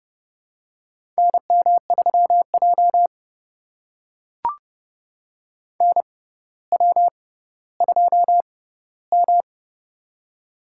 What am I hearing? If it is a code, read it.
Morse code, heard as a keyed tone, about 30 wpm, spaced as for 15 wpm, 700 Hz. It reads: NM3J NW2M
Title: USA Call Signs - 4 Characters - 6x Character Spacing - 30wpm